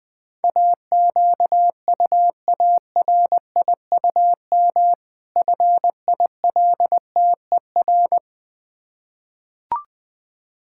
Morse code, heard as a keyed tone, about 20 wpm, 700 Hz. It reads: AQUARIUM FILTER
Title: Difficult Words with Hints - 20wpm